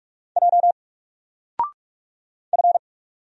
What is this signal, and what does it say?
Morse code, heard as a keyed tone, about 48 wpm, 700 Hz. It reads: J F
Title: Single Character (Letter:Number:Punctuation:Prosign) Alt 45wpm